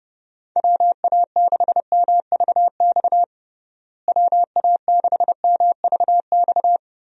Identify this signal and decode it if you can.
Morse code, heard as a keyed tone, about 30 wpm, 700 Hz. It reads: WA6MVX WA6MVX
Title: USA Call Signs - 6 Characters - Repeated 2x - 30wpm